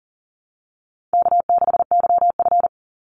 Morse code, heard as a keyed tone, about 40 wpm, 700 Hz. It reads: K6YF
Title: Easy Call Signs - 40wpm